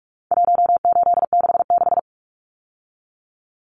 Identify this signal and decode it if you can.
Morse code, heard as a keyed tone, about 44 wpm, 700 Hz. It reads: 1866
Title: Numbers - 4-Digit 45wpm